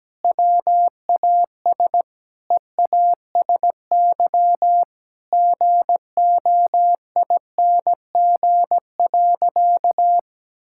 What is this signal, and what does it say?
Morse code, heard as a keyed tone, about 17 wpm, 700 Hz. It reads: WAS EASY GOING.
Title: Sentences from Top 1000 Words - Review Part 5 - 17wpm